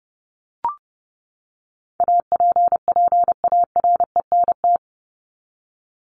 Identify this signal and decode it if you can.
Morse code, heard as a keyed tone, about 30 wpm, 700 Hz. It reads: APPARENT
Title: LY Suffix - Top 100 Words with Base Words 30wpm